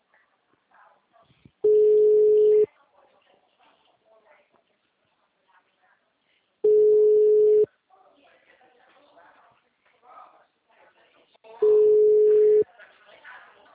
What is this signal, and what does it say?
Ringback tone (Europe, 425 Hz), with speech.